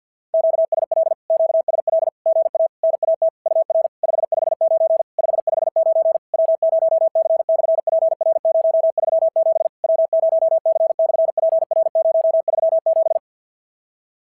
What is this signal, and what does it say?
Morse code, heard as a keyed tone, about 46 wpm, 660 Hz. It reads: QSL QSL GA NAT UR 559 559 W0C/PR037 W0C/PR037
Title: SOTA QSO Practice 50wpm